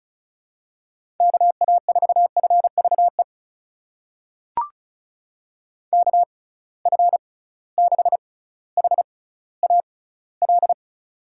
Morse code, heard as a keyed tone, about 36 wpm, spaced as for 18 wpm, 690 Hz. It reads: KA4FVE KF6HAL
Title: USA Call Signs - 6 Characters - 6x Character Spacing - 35wpm